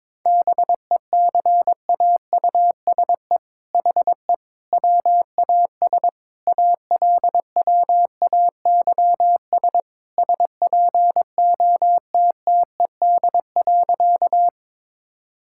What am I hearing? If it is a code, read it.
Morse code, heard as a keyed tone, about 22 wpm, 700 Hz. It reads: BECAUSE HE WAS ALWAYS SPOTTED.
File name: All-Age Appropriate Jokes - 22wpm